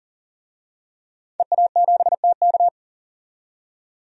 Morse code, heard as a keyed tone, about 40 wpm, 700 Hz. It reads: EA7TK
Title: DXCC-100 Call Signs - 5 Characters - 40wpm